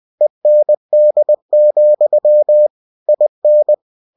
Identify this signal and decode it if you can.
Morse code, heard as a keyed tone, about 20 wpm, 600 Hz. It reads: END, IN
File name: cw-books / winnie-the-pooh / winnie-20wpm-page-0079